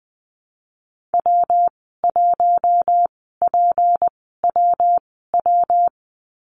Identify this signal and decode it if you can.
Morse code, heard as a keyed tone, about 20 wpm, spaced as for 15 wpm, 700 Hz. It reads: W1PWW
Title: 173 Lesson 28g - Number 9 - Call Signs 20wpm